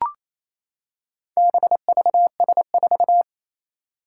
Morse code, heard as a keyed tone, about 28 wpm, 700 Hz. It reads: BVS4
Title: Four Challenging Letter-Number Combinations 28wpm